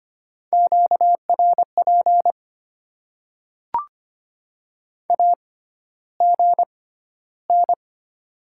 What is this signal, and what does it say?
Morse code, heard as a keyed tone, about 25 wpm, spaced as for 13 wpm, 700 Hz. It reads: QRP AGN
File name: Top 90 QSO Elements - 6x Character Spacing - 25wpm